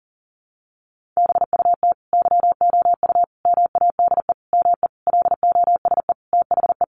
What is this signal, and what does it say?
Morse code, heard as a keyed tone, about 40 wpm, 700 Hz. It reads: BUT YOU MADE ME LOSE THE
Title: Sentences from Top 1300 Words 40wpm